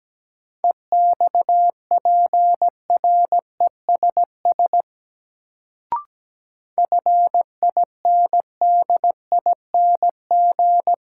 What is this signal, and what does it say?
Morse code, heard as a keyed tone, about 17 wpm, 700 Hz. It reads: EXPRESS FINDING
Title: Sentences from Top 1100 Words - New Word Drill Alt - 17wpm